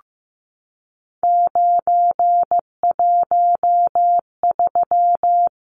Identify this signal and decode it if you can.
Morse code, heard as a keyed tone, about 15 wpm, 700 Hz. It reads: 913
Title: Numbers - 3-Digit 15wpm